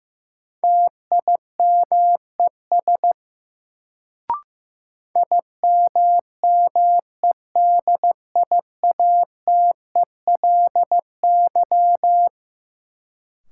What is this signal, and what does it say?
Morse code, heard as a keyed tone, about 15 wpm, 700 Hz. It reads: TIMES IMMEDIATELY
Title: Sentences from Top 1100 Words - New Word Drill Alt - 15wpm